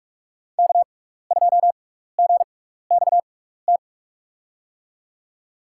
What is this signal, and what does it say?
Morse code, heard as a keyed tone, about 42 wpm, spaced as for 15 wpm, 700 Hz. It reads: K2GXT
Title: USA Call Signs - 5 Characters - 6x Character Spacing - 45wpm